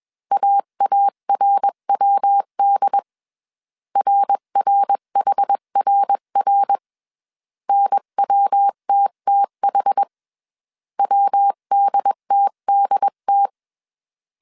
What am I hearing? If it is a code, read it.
Morse code, heard as a keyed tone, about 21 wpm, 800 Hz. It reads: AARWD RRHRR NWTTH WDTDT
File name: 20wpm